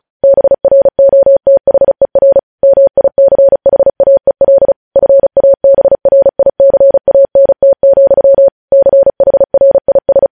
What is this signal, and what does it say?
Morse code, heard as a keyed tone, about 36 wpm, 550 Hz. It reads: BROTHER MICHAEL FABRICANT, CHRIS
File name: podcast / News headlines-1x-35x35